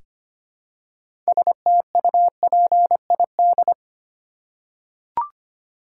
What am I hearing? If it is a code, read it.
Morse code, heard as a keyed tone, about 25 wpm, 700 Hz. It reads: STUPID
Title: Six Letter Words - 25wpm